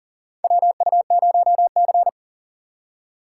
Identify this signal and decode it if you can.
Morse code, heard as a keyed tone, about 40 wpm, 700 Hz. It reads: WU0C
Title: USA Call Signs - 4 Characters - 3x Character Spacing - 40wpm